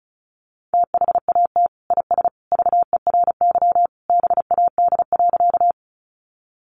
Morse code, heard as a keyed tone, about 34 wpm, 700 Hz. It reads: THAT IS VERY BAD.
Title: Sentences from Top 700 Words 35wpm